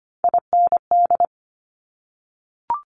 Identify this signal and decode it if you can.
Morse code, heard as a keyed tone, about 25 wpm, 700 Hz. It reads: IND
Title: Top Three Letter Combinations in Words 25wpm